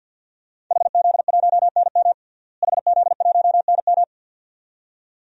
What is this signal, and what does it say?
Morse code, heard as a keyed tone, about 51 wpm, 700 Hz. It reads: HZ1NM HZ1NM
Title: DXCC-100 Call Signs - 5 Characters - Repeated 2x - 50wpm